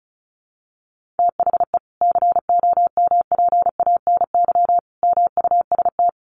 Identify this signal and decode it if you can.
Morse code, heard as a keyed tone, about 34 wpm, 700 Hz. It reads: THE COMPANY MUST
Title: Sentences from Top 1400 Words 35wpm